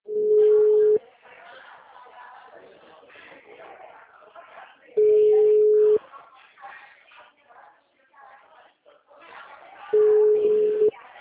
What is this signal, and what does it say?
Ringback tone (Europe, 425 Hz), with speech.